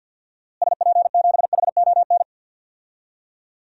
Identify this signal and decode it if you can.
Morse code, heard as a keyed tone, about 47 wpm, 700 Hz. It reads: SP7HON
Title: DXCC-100 Call Signs - 6 Characters - Repeated 2x - 50wpm